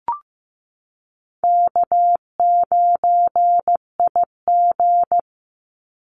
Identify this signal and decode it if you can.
Morse code, heard as a keyed tone, about 15 wpm, 700 Hz. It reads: K9IG
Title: Call Signs Repeated 2x - 15wpm